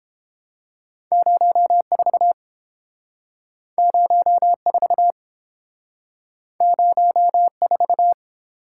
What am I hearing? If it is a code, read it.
Morse code, heard as a keyed tone, about 30 wpm, 700 Hz. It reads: 04 04 04